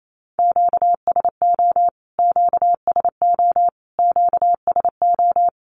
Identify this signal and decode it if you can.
Morse code, heard as a keyed tone, about 28 wpm, 700 Hz. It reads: QSO QSO QSO
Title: Q-Codes Repeated 3x 28wpm